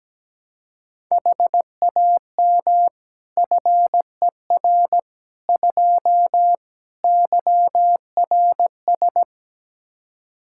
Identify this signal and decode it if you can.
Morse code, heard as a keyed tone, about 17 wpm, 700 Hz. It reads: HAM FER 2 YRS
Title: Practice QSO - Exchange 3 - More Info - 17wpm